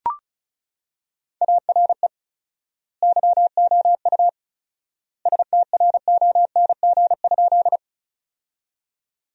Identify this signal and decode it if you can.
Morse code, heard as a keyed tone, about 36 wpm, 700 Hz. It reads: ARE YOU STRONG?
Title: 4x Word-Spacing - Sentences from Top 500 Words - 35wpm